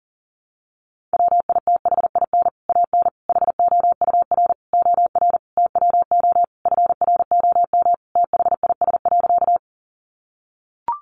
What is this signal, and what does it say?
Morse code, heard as a keyed tone, about 40 wpm, 700 Hz. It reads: WITHIN AN HOUR OR TWO FROM THIS.
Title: Sentences from Top 700 Words 40wpm